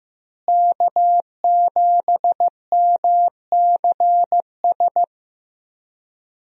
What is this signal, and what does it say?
Morse code, heard as a keyed tone, about 15 wpm, 700 Hz. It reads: K7MCS
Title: USA Call Signs - 5 Characters - Repeated 3x - 15wpm